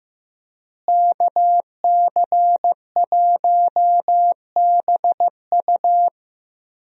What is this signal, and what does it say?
Morse code, heard as a keyed tone, about 16 wpm, 700 Hz. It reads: KC1BU
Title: USA Call Signs - 5 Characters - Repeated 3x - 15wpm